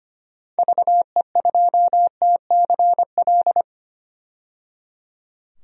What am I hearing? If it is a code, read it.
Morse code, heard as a keyed tone, about 25 wpm, 700 Hz. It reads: VE2TCL
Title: DXCC-100 Call Signs - 6 Characters - 25wpm